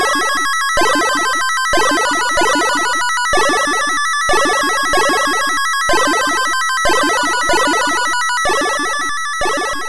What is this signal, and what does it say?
There is no code.